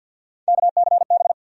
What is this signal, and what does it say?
Morse code, heard as a keyed tone, about 51 wpm, 700 Hz. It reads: KCB
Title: USA Call Sign Postfixes - 3 Letters 50wpm